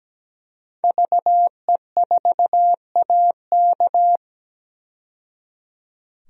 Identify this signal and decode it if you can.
Morse code, heard as a keyed tone, about 17 wpm, 700 Hz. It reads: VE4AK
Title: DXCC-100 Call Signs - 5 Characters - 17wpm